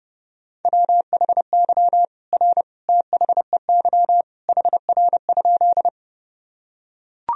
Morse code, heard as a keyed tone, about 30 wpm, 700 Hz. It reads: WHY R THEY HR?
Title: Sentences from Top 200 Words and Abbreviations 30wpm